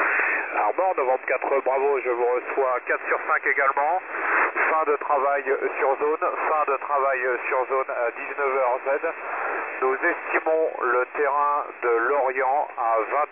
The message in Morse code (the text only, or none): none